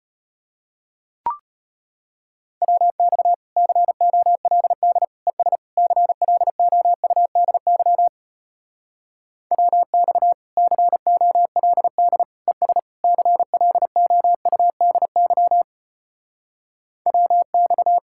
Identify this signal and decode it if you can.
Morse code, heard as a keyed tone, about 33 wpm, 700 Hz. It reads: WX COLD ES CLOUDY WX COLD ES CLOUDY WX